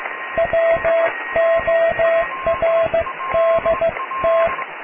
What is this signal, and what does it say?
Morse code, heard as a keyed tone, about 16 wpm, 660 Hz, with radio noise. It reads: WORDT
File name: cw / tekst(a)+QRM